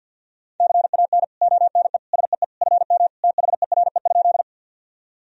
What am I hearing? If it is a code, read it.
Morse code, heard as a keyed tone, about 45 wpm, 700 Hz. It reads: CAN ONE SEE FM THERE?
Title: Sentences from Top 100 Words and Abbreviations 50wpm